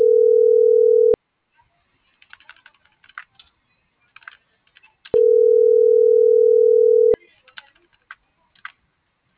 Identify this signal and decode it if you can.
Ringback tone (North America), 440 + 480 Hz.